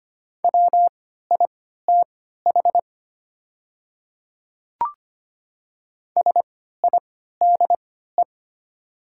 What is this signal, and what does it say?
Morse code, heard as a keyed tone, about 25 wpm, spaced as for 14 wpm, 690 Hz. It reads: WITH SIDE
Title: 3x Character Spacing - Top 200 Words - 25wpm